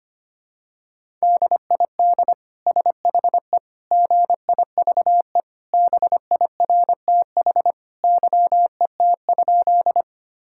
Morse code, heard as a keyed tone, about 25 wpm, 700 Hz. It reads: DID SHE GIVE BIRTH YET?
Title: Sentences from Top 1000 Words - Review Part 5 - 25wpm